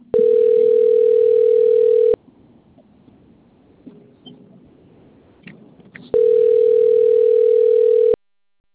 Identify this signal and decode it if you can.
Ringback tone (North America), 440 + 480 Hz.